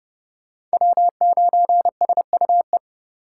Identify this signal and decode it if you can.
Morse code, heard as a keyed tone, about 32 wpm, 700 Hz. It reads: W9SUE